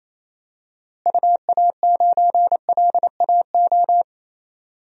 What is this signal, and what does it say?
Morse code, heard as a keyed tone, about 27 wpm, 700 Hz. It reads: UA9LAO
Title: DXCC-100 Call Signs - 6 Characters - Repeated 2x - 28wpm